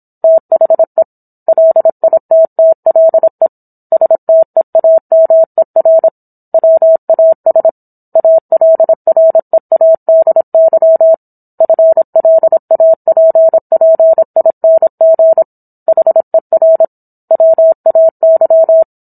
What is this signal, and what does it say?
Morse code, heard as a keyed tone, about 27 wpm, 650 Hz. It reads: THE LITTLE STEAMER WAS ALREADY FLAPPING HER WAY